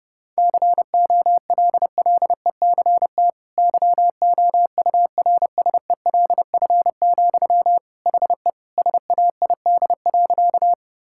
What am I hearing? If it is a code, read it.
Morse code, heard as a keyed tone, about 30 wpm, 700 Hz. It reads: COLLECT YOURSELF, HE SAID.
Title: Sentences from Top 1600 Words 30wpm